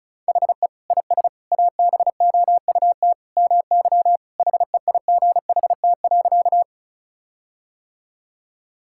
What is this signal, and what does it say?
Morse code, heard as a keyed tone, about 36 wpm, 700 Hz. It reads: HE IS ABOUT MY HEIGHT.
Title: Sentences from Top 1600 Words 35wpm